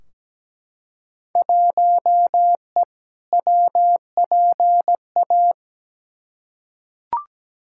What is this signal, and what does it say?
Morse code, heard as a keyed tone, about 17 wpm, 700 Hz. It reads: 1E WPA
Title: Field Day Exchanges - 17wpm